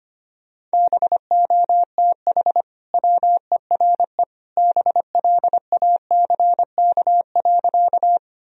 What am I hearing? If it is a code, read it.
Morse code, heard as a keyed tone, about 25 wpm, 700 Hz. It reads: BOTH WERE BLACK.
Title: Sentences from Top 1000 Words - Review Part 2 - 25wpm